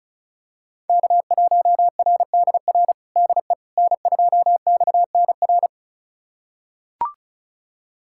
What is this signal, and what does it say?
Morse code, heard as a keyed tone, about 34 wpm, 700 Hz. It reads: K1RDR DE N2XNR
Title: USA Callsign DE Callsign - 5 Characters - 35wpm